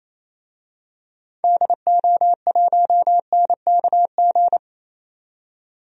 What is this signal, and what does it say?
Morse code, heard as a keyed tone, about 28 wpm, 700 Hz. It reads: DO1NKG
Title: DXCC-100 Call Signs - 6 Characters - 28wpm